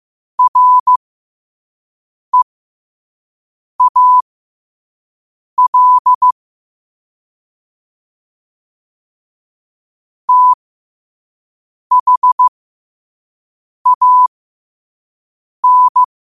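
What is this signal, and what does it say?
Morse code, heard as a keyed tone, about 15 wpm, spaced as for 5 wpm, 1000 Hz. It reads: REAL THAN
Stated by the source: Morse code quote of the day at 5 wpm.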